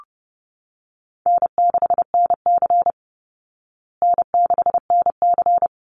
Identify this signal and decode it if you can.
Morse code, heard as a keyed tone, about 30 wpm, 700 Hz. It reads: N6NC N6NC
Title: Easy Call Signs Repeated 2x - 30wpm